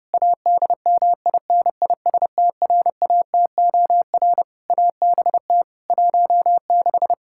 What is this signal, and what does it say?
Morse code, heard as a keyed tone, about 30 wpm, 700 Hz. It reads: ADMINISTRATOR ABT 16
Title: Practice QSO - Exchange 3 - More Info - 30wpm